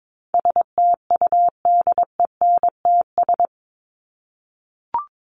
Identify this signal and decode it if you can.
Morse code, heard as a keyed tone, about 22 wpm, 700 Hz. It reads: STUDENTS
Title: ES:S Suffix - Top 100 Words with Base Words 22wpm